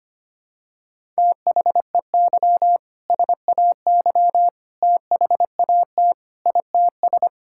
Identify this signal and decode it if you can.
Morse code, heard as a keyed tone, about 25 wpm, 700 Hz. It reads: THEY SAY THAT ITS
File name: practice / 25wpm / Sentences with 2-Word Phrases - 25wpm